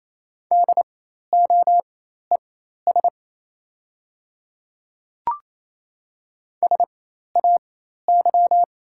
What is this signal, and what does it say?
Morse code, heard as a keyed tone, about 28 wpm, spaced as for 13 wpm, 690 Hz. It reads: DOES SAY
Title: 4x Character Spacing - Top 200 Words - 28wpm